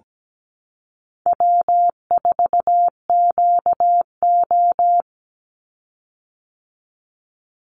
Morse code, heard as a keyed tone, about 17 wpm, 700 Hz. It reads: W4QO
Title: Easy Call Signs Repeated 2x - 17wpm